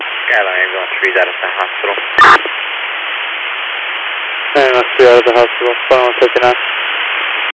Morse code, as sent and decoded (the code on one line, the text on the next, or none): none